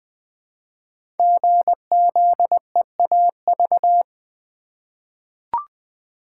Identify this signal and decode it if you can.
Morse code, heard as a keyed tone, about 20 wpm, 700 Hz. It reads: GZEAV
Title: Code Groups - 5 Letter:Numbers - 20wpm